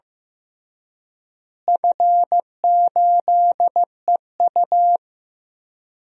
Morse code, heard as a keyed tone, about 15 wpm, 700 Hz. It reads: F8EU
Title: DXCC-100 Call Signs - 4 Characters - Repeated 2x - 15wpm